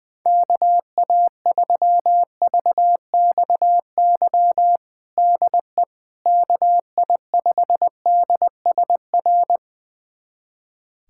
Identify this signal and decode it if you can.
Morse code, heard as a keyed tone, about 21 wpm, 700 Hz. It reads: KA3VXY DE KI5DSR
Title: USA Callsign DE Callsign 20wpm